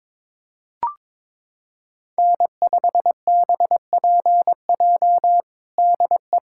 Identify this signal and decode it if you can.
Morse code, heard as a keyed tone, about 22 wpm, 700 Hz. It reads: N5BPJ DE
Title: K1USN SST QSO Practice 22wpm